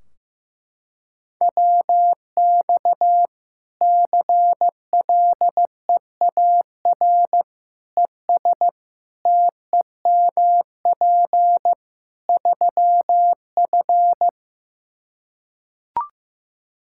Morse code, heard as a keyed tone, about 15 wpm, 700 Hz. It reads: WX CLEAR ES TEMP 3F
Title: Practice QSO - Exchange 2 - Info - 15wpm